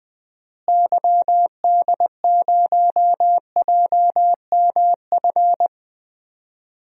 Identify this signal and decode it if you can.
Morse code, heard as a keyed tone, about 20 wpm, 700 Hz. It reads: YD0JMF
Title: DXCC-100 Call Signs - 6 Characters - Repeated 3x - 20wpm